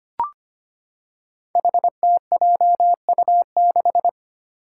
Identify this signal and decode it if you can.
Morse code, heard as a keyed tone, about 25 wpm, 700 Hz. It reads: HTJU6
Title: Five Challenging Letter-Number Combinations 25wpm